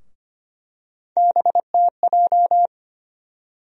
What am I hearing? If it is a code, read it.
Morse code, heard as a keyed tone, about 25 wpm, 700 Hz. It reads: BTJ
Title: USA Call Sign Postfixes - 3 Letters 25wpm